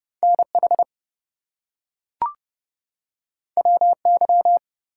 Morse code, heard as a keyed tone, about 30 wpm, 700 Hz. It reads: NH WY
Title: US States and Canadian Provinces 30wpm